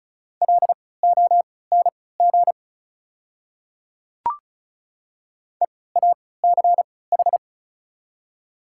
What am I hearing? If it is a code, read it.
Morse code, heard as a keyed tone, about 35 wpm, spaced as for 20 wpm, 690 Hz. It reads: LONG EACH
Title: 3x Character Spacing - Top 100 Words - 35wpm